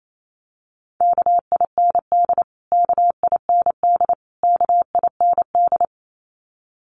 Morse code, heard as a keyed tone, about 28 wpm, 700 Hz. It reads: KIND KIND KIND
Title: Top 500 Words Repeated 28wpm